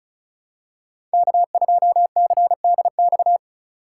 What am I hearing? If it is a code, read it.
Morse code, heard as a keyed tone, about 36 wpm, 700 Hz. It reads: K2CDX
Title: K1USN SST QSO Practice - 35wpm